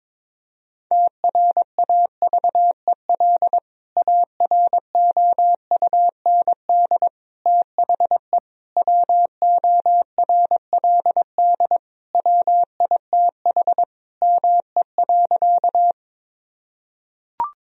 Morse code, heard as a keyed tone, about 22 wpm, 700 Hz. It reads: TRAVEL AROUND THE WORLD WITH ME.
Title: Sentences from Top 1000 Words - Review Part 3 - 22wpm